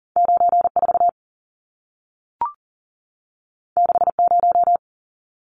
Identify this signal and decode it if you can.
Morse code, heard as a keyed tone, about 40 wpm, 700 Hz. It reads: 94 60
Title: Numbers - 2-Digit 40wpm